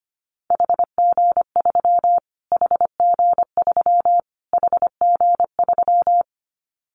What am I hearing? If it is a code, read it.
Morse code, heard as a keyed tone, about 25 wpm, 700 Hz. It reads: HG3 HG3 HG3
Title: Callsign Prefixes for Top 25 DXCC - Repeated 3x - 25wpm